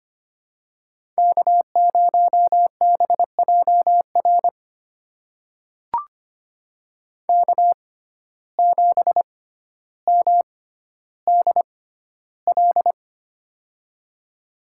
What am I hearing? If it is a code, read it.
Morse code, heard as a keyed tone, about 25 wpm, spaced as for 13 wpm, 700 Hz. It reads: K0BJR K7MDL
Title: USA Call Signs - 5 Characters - 6x Character Spacing - 25wpm